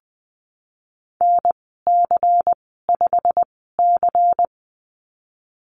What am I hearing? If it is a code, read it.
Morse code, heard as a keyed tone, about 20 wpm, spaced as for 15 wpm, 700 Hz. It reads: NC5C
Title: 167 Lesson 27g - Number 7 - Call Signs 20wpm